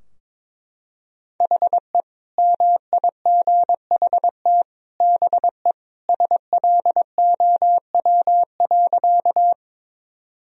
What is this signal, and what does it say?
Morse code, heard as a keyed tone, about 21 wpm, 700 Hz. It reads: HE MIGHT BE SLOW.
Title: Sentences from Top 1000 Words - Review Part 2 - 22wpm